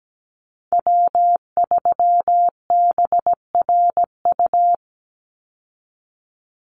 Morse code, heard as a keyed tone, about 17 wpm, 700 Hz. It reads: W3BRU
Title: USA Call Signs 17wpm